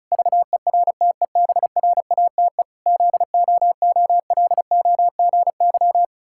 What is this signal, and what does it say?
Morse code, heard as a keyed tone, about 36 wpm, 700 Hz. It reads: VERTEBRATE ZOOLOGY
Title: Pangram Challenge v2 - 35wpm